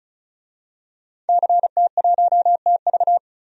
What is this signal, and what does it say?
Morse code, heard as a keyed tone, about 36 wpm, 700 Hz. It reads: CT1TV
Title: DXCC-100 Call Signs - 5 Characters - 35wpm